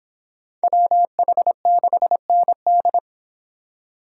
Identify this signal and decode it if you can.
Morse code, heard as a keyed tone, about 31 wpm, 700 Hz. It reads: WH6ND